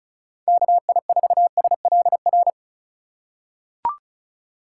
Morse code, heard as a keyed tone, about 36 wpm, 700 Hz. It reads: KI4SLR
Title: USA Call Signs - 2x Character Spacing - 35wpm